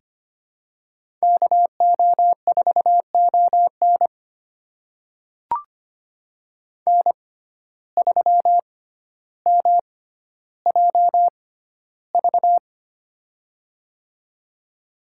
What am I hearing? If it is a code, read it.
Morse code, heard as a keyed tone, about 25 wpm, spaced as for 13 wpm, 700 Hz. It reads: KO4ON N3MJV
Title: USA Call Signs - 5 Characters - 6x Character Spacing - 25wpm